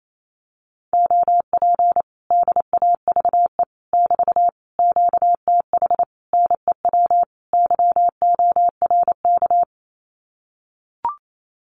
Morse code, heard as a keyed tone, about 28 wpm, 700 Hz. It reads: OP DAVE <BT> QTH NEW YORK
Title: Common QSO Phrases Encore 28wpm